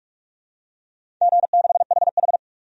Morse code, heard as a keyed tone, about 44 wpm, 700 Hz. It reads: G6HH
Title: DXCC-100 Call Signs - 4 Characters - 45wpm